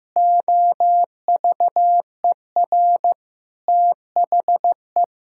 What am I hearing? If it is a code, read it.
Morse code, heard as a keyed tone, about 16 wpm, 700 Hz. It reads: OVER THE
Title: Pangram Challenge v2 - 15wpm